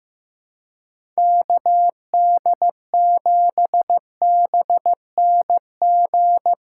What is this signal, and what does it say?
Morse code, heard as a keyed tone, about 16 wpm, 700 Hz. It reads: KD7BNG
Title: USA Callsign DE Callsign - 6 Characters - 15wpm